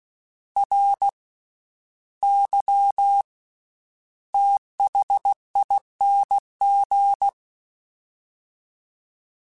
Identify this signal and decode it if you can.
Morse code, heard as a keyed tone, about 16 wpm, spaced as for 10 wpm, 800 Hz. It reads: RYTHING
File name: Special INGs / Special INGs-mp3s / VerySlowToFaster Ing @15